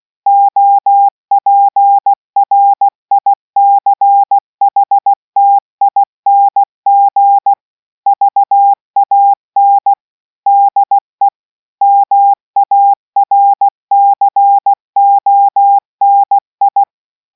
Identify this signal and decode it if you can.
Morse code, heard as a keyed tone, about 16 wpm, 800 Hz. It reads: OPRICHTING VAN DE MARCONI